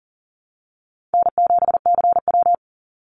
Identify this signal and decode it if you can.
Morse code, heard as a keyed tone, about 38 wpm, 700 Hz. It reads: N7CW
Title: NA SOTA Activator Callsigns 40wpm